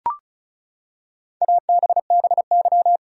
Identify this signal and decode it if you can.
Morse code, heard as a keyed tone, about 34 wpm, 700 Hz. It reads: ABBY
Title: USA Common Names - Max 5 Letters - 35wpm